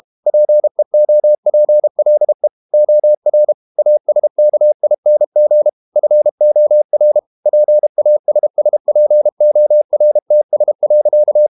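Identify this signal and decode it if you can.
Morse code, heard as a keyed tone, about 32 wpm, 600 Hz. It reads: PEOPLE OR ASKING FOR PASSPORTS.
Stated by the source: Morse code practice at 32 wpm from CW med Gnister, reading Ken Follett's novel Never in English.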